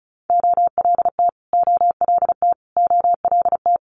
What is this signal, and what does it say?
Morse code, heard as a keyed tone, about 36 wpm, 700 Hz. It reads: OLT OLT OLT
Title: Multi-letter Phonemes Repeated 3x 35wpm